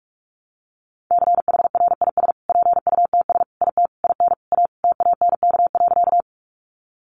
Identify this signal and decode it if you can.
Morse code, heard as a keyed tone, about 44 wpm, 700 Hz. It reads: CHRIS PUTS IT IN A TANK.
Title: Additional Easy Phrases 45wpm